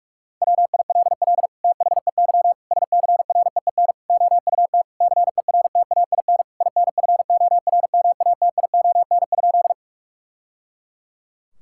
Morse code, heard as a keyed tone, about 46 wpm, 700 Hz. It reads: WILL THEY SCREEN OUT CERTAIN INFORMATION?
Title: Sentences from Top 1000 Words - Review Part 6 - 45wpm